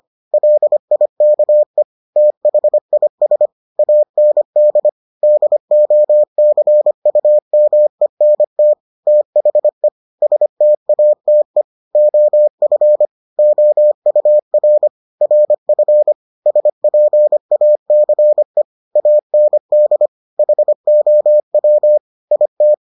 Morse code, heard as a keyed tone, about 25 wpm, 600 Hz. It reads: LIKE THIS AND DOCUMENT THE STATE OF OUR RF SPACE AND HOW IT